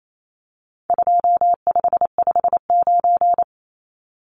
Morse code, heard as a keyed tone, about 28 wpm, 700 Hz. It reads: 2559
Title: Numbers - 4-Digit 28wpm